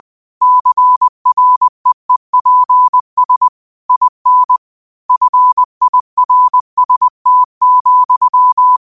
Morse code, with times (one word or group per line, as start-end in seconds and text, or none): 0.4-3.5 CREEPS
3.9-4.6 IN
5.1-8.8 FIRST,